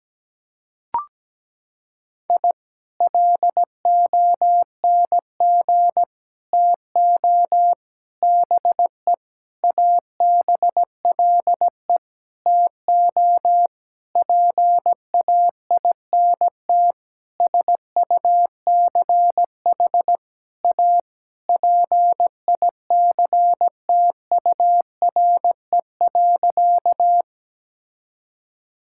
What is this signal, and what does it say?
Morse code, heard as a keyed tone, about 17 wpm, 700 Hz. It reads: I LONG TO BE ABLE TO PAINT SUCH A PICTURE.
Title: Sentences from Top 1000 Words - Review Part 6 - 17wpm